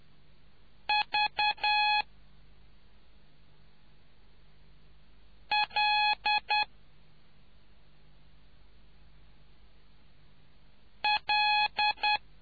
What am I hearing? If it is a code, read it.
Morse code, heard as a keyed tone, about 10 wpm, 840 Hz. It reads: V L L